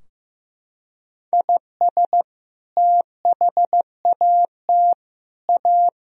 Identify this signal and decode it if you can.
Morse code, heard as a keyed tone, about 15 wpm, 700 Hz. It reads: IS THAT A
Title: Sentences from Top 1000 Words - Review Part 2 - 15wpm